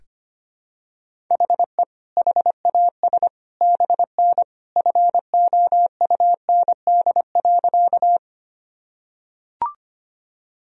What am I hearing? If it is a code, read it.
Morse code, heard as a keyed tone, about 25 wpm, 700 Hz. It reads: HE HAS BN FOUND.
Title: Sentences from Top 300 Words and Abbreviations 25wpm